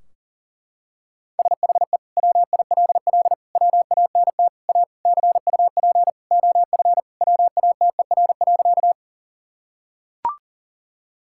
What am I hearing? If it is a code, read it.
Morse code, heard as a keyed tone, about 40 wpm, 700 Hz. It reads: SHE WILL WANT A CUP OF WATER.
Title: Sentences from Top 1600 Words 40wpm